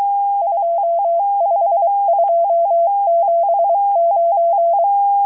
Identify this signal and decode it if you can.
Morse code, heard as a keyed tone, about 23 wpm, 680 Hz. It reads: 25279
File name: HM02- -0437utc-7351khz